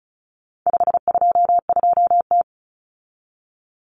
Morse code, heard as a keyed tone, about 34 wpm, 700 Hz. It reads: 522T
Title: Common Cut Numbers - 4-Digit 35wpm